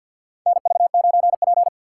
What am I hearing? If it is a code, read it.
Morse code, heard as a keyed tone, about 53 wpm, 700 Hz. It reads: NV9P
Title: USA Call Signs - 4x Character Spacing - 50wpm